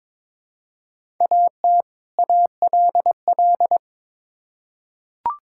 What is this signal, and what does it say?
Morse code, heard as a keyed tone, about 22 wpm, 700 Hz. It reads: AT ALL
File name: Build A Phrase - 2-Word Phrases - 22wpm